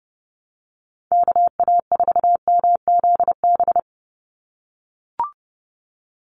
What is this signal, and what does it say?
Morse code, heard as a keyed tone, about 30 wpm, 700 Hz. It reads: KA4MZB
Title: USA Call Signs 30wpm